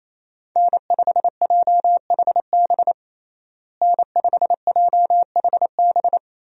Morse code, heard as a keyed tone, about 27 wpm, 700 Hz. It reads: N5JHB N5JHB
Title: USA Call Signs - 5 Characters - Repeated 3x - 28wpm